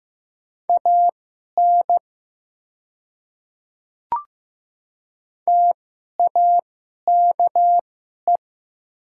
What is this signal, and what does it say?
Morse code, heard as a keyed tone, about 15 wpm, spaced as for 11 wpm, 700 Hz. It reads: AN TAKE
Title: 2x Character Spacing - Top 200 Words - 15wpm